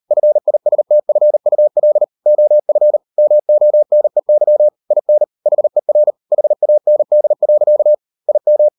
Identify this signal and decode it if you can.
Morse code, heard as a keyed tone, about 38 wpm, 600 Hz. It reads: FISTFUL OF MONEY IN HER HAND. IM